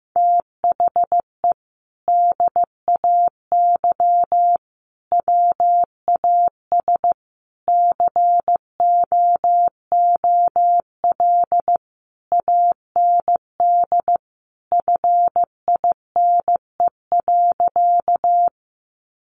Morse code, heard as a keyed tone, about 15 wpm, 700 Hz. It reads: THE DAY WAS COOL AND FINE.
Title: Sentences from Top 1200 Words 15wpm